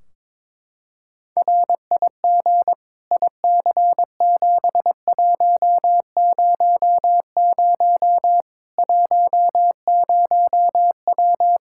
Morse code, heard as a keyed tone, about 22 wpm, 700 Hz. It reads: RIG IC7100 10W
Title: Practice QSO - Exchange 2 - Info - 22wpm